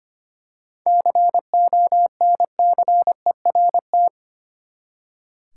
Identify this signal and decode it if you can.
Morse code, heard as a keyed tone, about 25 wpm, 700 Hz. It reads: CONCERT
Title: Seven Letter Words - 25wpm